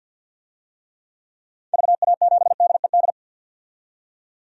Morse code, heard as a keyed tone, about 47 wpm, 700 Hz. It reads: VA7DED
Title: DXCC-100 Call Signs - 6 Characters - 50wpm